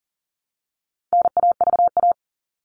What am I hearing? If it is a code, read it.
Morse code, heard as a keyed tone, about 40 wpm, 700 Hz. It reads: NAVA
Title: All Cut Numbers - 4-Digit 40wpm